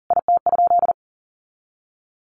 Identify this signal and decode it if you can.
Morse code, heard as a keyed tone, about 40 wpm, 700 Hz. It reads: IT?
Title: Sentences from Top 200 Words 40wpm